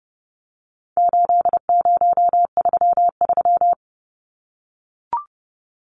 Morse code, heard as a keyed tone, about 30 wpm, 700 Hz. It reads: 8033
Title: Numbers - 4-Digit 30wpm